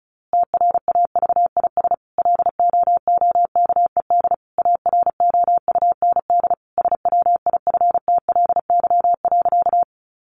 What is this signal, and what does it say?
Morse code, heard as a keyed tone, about 34 wpm, 700 Hz. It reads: TRAVIS LOOKED AROUND SWIFTLY.
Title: A Sound Of Thunder 35wpm